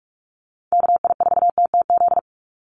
Morse code, heard as a keyed tone, about 44 wpm, 700 Hz. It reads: KI4TTZ
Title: USA Call Signs 45wpm